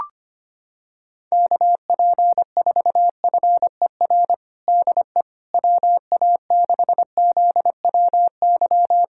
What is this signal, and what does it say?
Morse code, heard as a keyed tone, about 25 wpm, 700 Hz. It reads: KP4FER DE WA6ZWY
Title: USA Callsign DE Callsign - 6 Characters - 25wpm